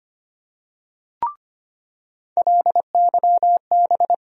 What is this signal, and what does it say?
Morse code, heard as a keyed tone, about 25 wpm, 700 Hz. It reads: LYB
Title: USA Call Sign Postfixes - 3 Letters 25wpm